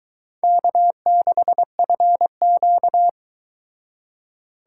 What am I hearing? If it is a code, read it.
Morse code, heard as a keyed tone, about 22 wpm, 700 Hz. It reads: K6FQ